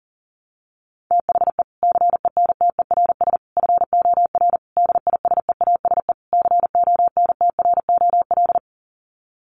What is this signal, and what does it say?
Morse code, heard as a keyed tone, about 40 wpm, 700 Hz. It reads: THE CENTERS FOR DISEASE CONTROL
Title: Sets of 5 Words Encore2 40wpm